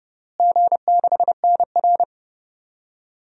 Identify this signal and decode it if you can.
Morse code, heard as a keyed tone, about 30 wpm, 700 Hz. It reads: G6NR
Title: DXCC-100 Call Signs - 4 Characters - Repeated 3x - 30wpm